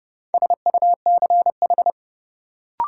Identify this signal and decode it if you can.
Morse code, heard as a keyed tone, about 30 wpm, 700 Hz. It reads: SUCH
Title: Sentences from Top 200 Words - New Word Drill - 30wpm